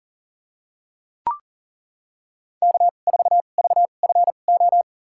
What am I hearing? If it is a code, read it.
Morse code, heard as a keyed tone, about 40 wpm, spaced as for 29 wpm, 700 Hz. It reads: K4VFO
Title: USA Call Signs - 5 Characters - 2x Character Spacing - 40wpm